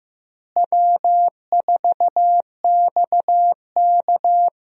Morse code, heard as a keyed tone, about 15 wpm, 700 Hz. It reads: W4XK
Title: USA Call Signs - 4 Characters - Repeated 3x - 15wpm